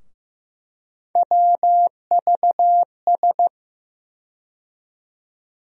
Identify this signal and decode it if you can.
Morse code, heard as a keyed tone, about 15 wpm, 700 Hz. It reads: WVS
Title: USA Call Sign Postfixes - 3 Letters 15wpm